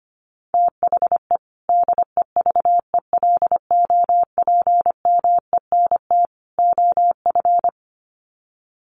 Morse code, heard as a keyed tone, about 25 wpm, 700 Hz. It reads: THE DEVELOPMENT OF
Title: Sets of 3 Words 25wpm